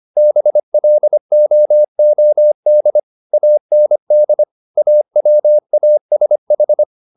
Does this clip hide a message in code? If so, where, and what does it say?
Yes: Morse code from 0.2 to 6.8 s, heard as a keyed tone, about 25 wpm, 600 Hz. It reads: BLOOD AND AWASH